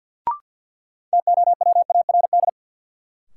Morse code, heard as a keyed tone, about 53 wpm, 700 Hz. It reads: TOWARD
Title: Sentences from Top 1000 Words - Review Part 6 - New Word Drill - 50wpm